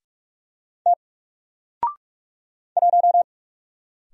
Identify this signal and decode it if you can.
Morse code, heard as a keyed tone, about 44 wpm, 700 Hz. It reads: T 1
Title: Single Character (Letter:Number:Punctuation:Prosign) 45wpm